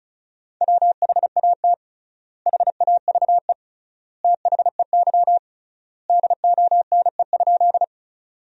Morse code, heard as a keyed tone, about 34 wpm, 700 Hz. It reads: WHAT HAVE THEY DONE?
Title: 3x Word-Spacing - Sentences from Top 500 Words - 35wpm